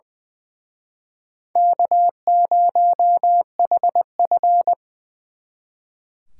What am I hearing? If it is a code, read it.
Morse code, heard as a keyed tone, about 23 wpm, 700 Hz. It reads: K0HF
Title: Speed Racing - USA 1x2 Call Signs 20wpm